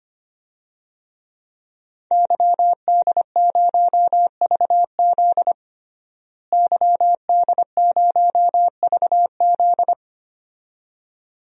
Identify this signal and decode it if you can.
Morse code, heard as a keyed tone, about 25 wpm, 700 Hz. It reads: YD0VZ YD0VZ
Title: DXCC-100 Call Signs - 5 Characters - Repeated 2x - 25wpm